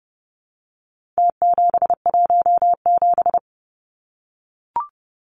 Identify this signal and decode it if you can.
Morse code, heard as a keyed tone, about 30 wpm, 700 Hz. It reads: T717
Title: Numbers - Contest Style - 4-Digit 30wpm